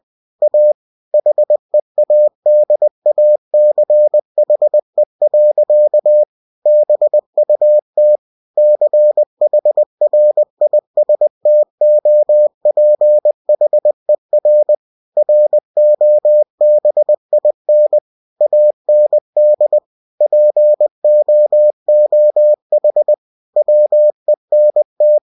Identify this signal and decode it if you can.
Morse code, heard as a keyed tone, about 20 wpm, 600 Hz. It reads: A HEADACHE. BUT CHRISTOPHER ROBIN AND POOH WENT
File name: cw-books / winnie-the-pooh / winnie-20wpm-page-0043